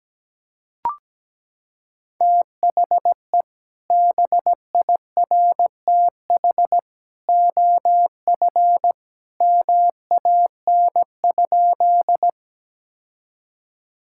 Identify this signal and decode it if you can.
Morse code, heard as a keyed tone, about 17 wpm, 700 Hz. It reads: THE BIRTH OF MAN?
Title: Sentences from Top 1000 Words - Review Part 5 - 17wpm